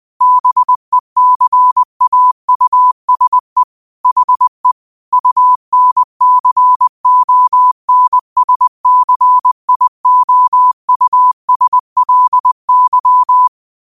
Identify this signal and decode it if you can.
Morse code, heard as a keyed tone, about 20 wpm, 1000 Hz. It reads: BECAUSE HE UNCONSCIOUSLY
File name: qotdfiles / 20wpm